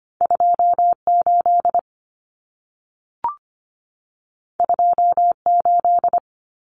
Morse code, heard as a keyed tone, about 25 wpm, 700 Hz. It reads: 28 28
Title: Numbers - 2-Digit 25wpm